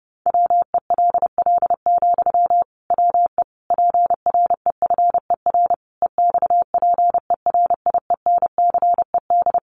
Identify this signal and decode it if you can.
Morse code, heard as a keyed tone, about 30 wpm, 700 Hz. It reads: WELL, WE PREFER EXPERIENCED
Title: Sentences from Top 1100 Words 30wpm